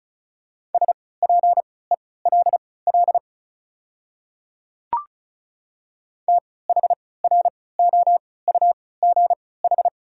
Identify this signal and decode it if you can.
Morse code, heard as a keyed tone, about 36 wpm, spaced as for 20 wpm, 700 Hz. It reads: SPELL THROUGH
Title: 3x Character Spacing - Top 300 Words - 35wpm